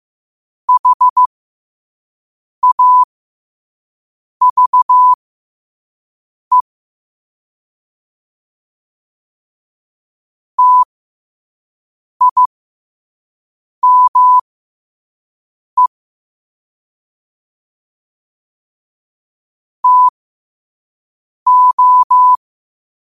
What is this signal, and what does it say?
Morse code, heard as a keyed tone, about 15 wpm, spaced as for 5 wpm, 1000 Hz. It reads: HAVE TIME TO
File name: qotdfiles / 5wpm